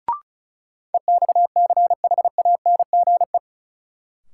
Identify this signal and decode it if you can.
Morse code, heard as a keyed tone, about 36 wpm, 700 Hz. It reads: EXCHANGE
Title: Sentences from Top 1000 Words - Review Part 6 - New Word Drill - 35wpm